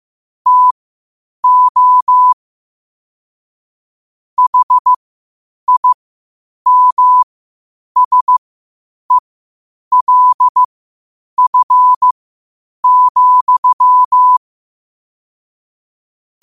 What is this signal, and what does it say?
Morse code, heard as a keyed tone, about 15 wpm, spaced as for 8 wpm, 1000 Hz. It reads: TO HIMSELF,
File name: qotdfiles / 8wpm